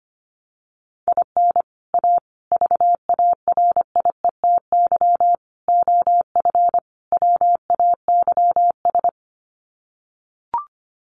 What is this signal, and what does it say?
Morse code, heard as a keyed tone, about 25 wpm, 700 Hz. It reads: IN A VARIETY OF WAYS
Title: Sets of 5 Words Encore 25wpm